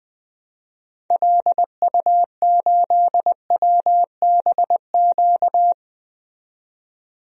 Morse code, heard as a keyed tone, about 21 wpm, 700 Hz. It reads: LU8WBQ
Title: DXCC-100 Call Signs - 6 Characters - 20wpm